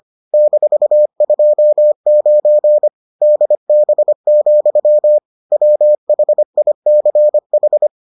Morse code, heard as a keyed tone, about 25 wpm, 600 Hz. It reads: -29 DB, WHICH